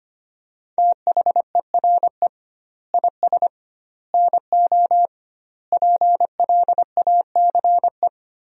Morse code, heard as a keyed tone, about 25 wpm, 700 Hz. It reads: THERE IS NO PLACE
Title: 2x Word-Spacing - Sentences from Top 200 Words - 25wpm